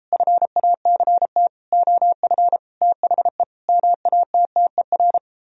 Morse code, heard as a keyed tone, about 38 wpm, 700 Hz. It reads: FACT OF THE MATTER